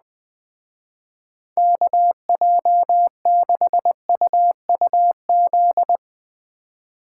Morse code, heard as a keyed tone, about 20 wpm, 700 Hz. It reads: KJ6UUZ